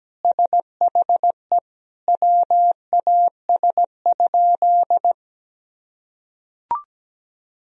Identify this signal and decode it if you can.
Morse code, heard as a keyed tone, about 17 wpm, 700 Hz. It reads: SHE WAS?
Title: Sentences from Top 200 Words and Abbreviations 17wpm